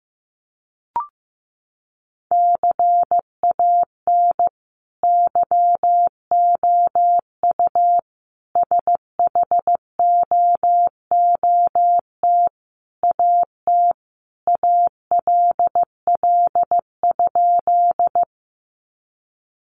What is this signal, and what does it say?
Morse code, heard as a keyed tone, about 15 wpm, 700 Hz. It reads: CAN YOU SHOOT AT ALL?
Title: Sentences from Top 1200 Words 15wpm